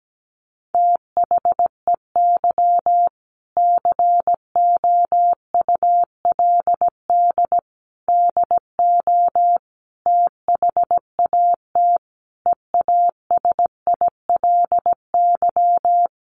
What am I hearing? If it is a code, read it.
Morse code, heard as a keyed tone, about 17 wpm, 700 Hz. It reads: THEY COULD DO THAT EASILY
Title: Sentences from Top 900 Words 17wpm